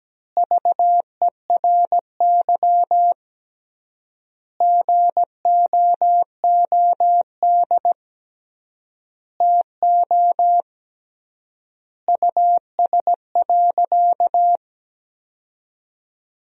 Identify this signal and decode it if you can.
Morse code, heard as a keyed tone, about 17 wpm, 700 Hz. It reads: VERY GOOD TO US.
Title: 3x Word-Spacing - Sentences from Top 200 Words - 17wpm